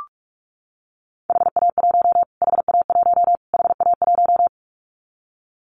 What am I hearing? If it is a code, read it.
Morse code, heard as a keyed tone, about 44 wpm, 700 Hz. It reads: HA1 HA1 HA1
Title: Callsign Prefixes for Top 25 DXCC - Repeated 3x - 45wpm